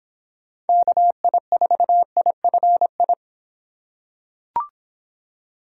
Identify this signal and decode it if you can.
Morse code, heard as a keyed tone, about 25 wpm, 700 Hz. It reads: KI4IFI